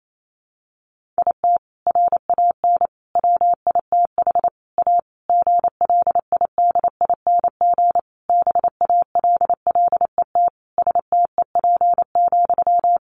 Morse code, heard as a keyed tone, about 28 wpm, 700 Hz. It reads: IT RAN WITH A GLIDING BALLET STEP,
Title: A Sound Of Thunder 28wpm